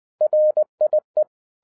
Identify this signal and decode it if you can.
Morse code, heard as a keyed tone, about 20 wpm, 610 Hz. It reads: RIE
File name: 2011-07-mp3 / n58 BACK FROM A BIG ACCIDENT 20wpm